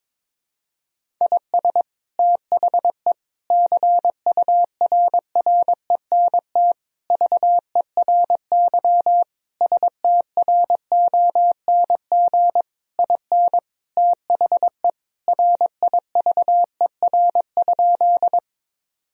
Morse code, heard as a keyed tone, about 22 wpm, 700 Hz. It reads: IS THE CURRENT VERY STRONG IN THE RIVER?
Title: Sentences from Top 1000 Words - Review Part 4 - 22wpm